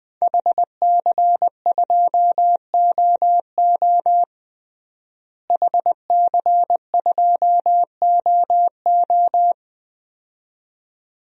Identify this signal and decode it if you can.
Morse code, heard as a keyed tone, about 20 wpm, 700 Hz. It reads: HC2OO HC2OO
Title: DXCC-100 Call Signs - 5 Characters - Repeated 2x - 20wpm